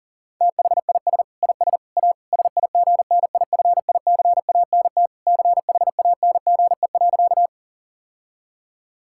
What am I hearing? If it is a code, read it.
Morse code, heard as a keyed tone, about 40 wpm, 700 Hz. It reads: THIS IS A SIGNIFICANT CHANGE.
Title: Sentences from Top 1000 Words - Review Part 6 - 40wpm